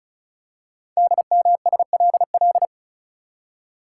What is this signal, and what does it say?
Morse code, heard as a keyed tone, about 36 wpm, 700 Hz. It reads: DMSLL
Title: Code Groups - 5 Letter:Numbers - 35wpm